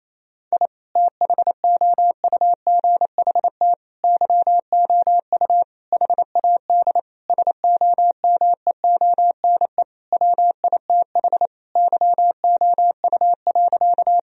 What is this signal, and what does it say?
Morse code, heard as a keyed tone, about 28 wpm, 700 Hz. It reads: I THOUGHT YOU HAD SOMEONE WITH YOU.
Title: Sentences from Top 1000 Words - Review Part 4 - 28wpm